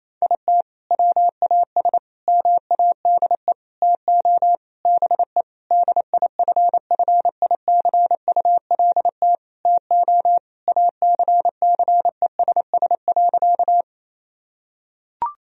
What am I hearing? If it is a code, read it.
Morse code, heard as a keyed tone, about 28 wpm, 700 Hz. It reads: IT WAS MADE TO BE DIFFICULT TO ACCESS.
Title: Sentences from Top 1000 Words - Review Part 4 - 28wpm